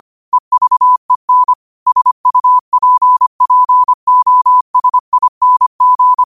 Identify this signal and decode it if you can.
Morse code, heard as a keyed tone, about 25 wpm, 1000 Hz. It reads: EVEN SUPPOSING
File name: qotdfiles / 25wpm